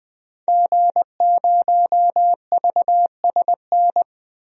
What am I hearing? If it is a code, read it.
Morse code, heard as a keyed tone, about 20 wpm, 700 Hz. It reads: G0VSN
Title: DXCC-100 Call Signs - 5 Characters - 20wpm